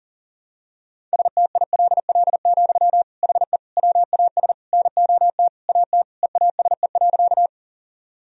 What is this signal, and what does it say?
Morse code, heard as a keyed tone, about 40 wpm, 700 Hz. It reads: STILL, HE WAS NOT AT EASE.
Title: Sentences from Top 1000 Words - Review Part 1 - 40wpm